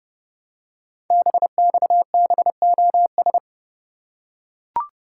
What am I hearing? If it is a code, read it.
Morse code, heard as a keyed tone, about 30 wpm, 700 Hz. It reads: BXBOS
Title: Code Groups - 5 Letters - 30wpm